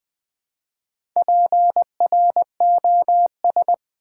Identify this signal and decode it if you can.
Morse code, heard as a keyed tone, about 20 wpm, 700 Hz. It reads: PROS
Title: Build A Word - 2 Letters At A Time - 8-Letter Words - 20wpm